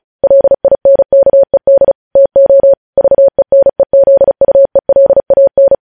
Morse code, heard as a keyed tone, about 36 wpm, 550 Hz. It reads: LINKED TO VENEZUELAN
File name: podcast / News headlines-1x-35x35